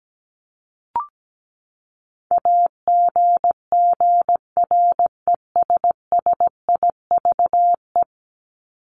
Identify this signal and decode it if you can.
Morse code, heard as a keyed tone, about 17 wpm, 700 Hz. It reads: AGGRESSIVE
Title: Top 1501-1750 Words 17wpm